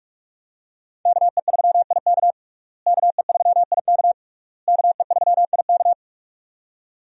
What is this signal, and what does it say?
Morse code, heard as a keyed tone, about 42 wpm, 700 Hz. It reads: KE3IK KE3IK KE3IK
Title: USA Call Signs - 5 Characters - Repeated 3x - 45wpm